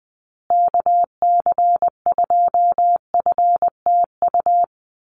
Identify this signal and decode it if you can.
Morse code, heard as a keyed tone, about 20 wpm, 700 Hz. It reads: KC2FTU
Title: USA Call Signs Repeated 2x 20wpm